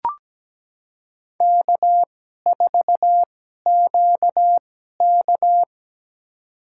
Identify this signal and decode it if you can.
Morse code, heard as a keyed tone, about 17 wpm, spaced as for 12 wpm, 700 Hz. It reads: K4QK
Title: USA Call Signs - 4 Characters - 2x Character Spacing - 17wpm